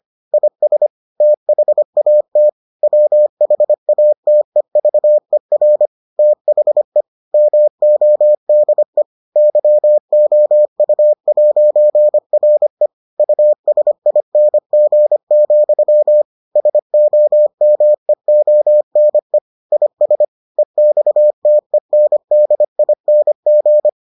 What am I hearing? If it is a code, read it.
Morse code, heard as a keyed tone, about 25 wpm, 600 Hz. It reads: IS THAT WHATEVER THE MODE YOU'RE USING, SOMEONE IS EXTENDING